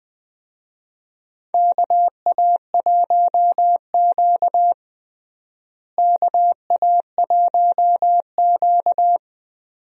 Morse code, heard as a keyed tone, about 20 wpm, 700 Hz. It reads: KA1Q KA1Q
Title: USA Call Signs - 4 Characters - Repeated 2x - 20wpm